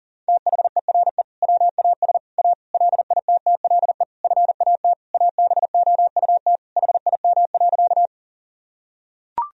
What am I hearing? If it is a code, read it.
Morse code, heard as a keyed tone, about 40 wpm, 700 Hz. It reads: THERE WAS A LITTLE FAT ABOUT HIM.
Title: Sentences from Top 1000 Words - Review Part 4 - 40wpm